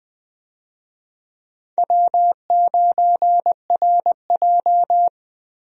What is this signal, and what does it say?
Morse code, heard as a keyed tone, about 20 wpm, 700 Hz. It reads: W9RJ
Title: USA Call Signs - 4 Characters - 20wpm